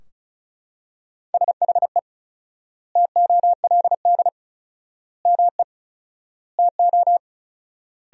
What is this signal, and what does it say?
Morse code, heard as a keyed tone, about 36 wpm, 700 Hz. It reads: SHE TOLD ME TO
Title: 4x Word-Spacing - Sentences from Top 400 Words - 35wpm